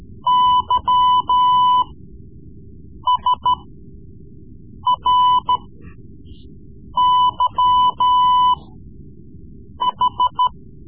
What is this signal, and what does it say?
Morse code, heard as a keyed tone, about 14 wpm, spaced as for 5 wpm, 1000 Hz. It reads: YSRYH
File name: Morse Lessons / Morse Lessons / Tape-3B